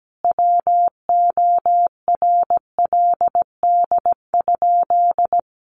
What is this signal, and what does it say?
Morse code, heard as a keyed tone, about 17 wpm, 700 Hz. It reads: WORLD?
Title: Sentences from Top 900 Words 17wpm